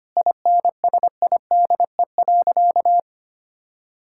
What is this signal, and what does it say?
Morse code, heard as a keyed tone, about 25 wpm, 700 Hz. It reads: INSIDE.
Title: Sentences from Top 1000 Words - Review Part 6 - 25wpm